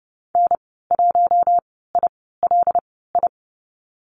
Morse code, heard as a keyed tone, about 30 wpm, spaced as for 17 wpm, 700 Hz. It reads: N1ILI
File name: Lesson 15g - Number 5 - Call Signs 30wpm